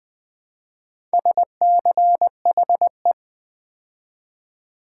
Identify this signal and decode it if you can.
Morse code, heard as a keyed tone, about 20 wpm, 700 Hz. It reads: SCHE
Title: Build A Word - 2 Letters At A Time - 8-Letter Words - 20wpm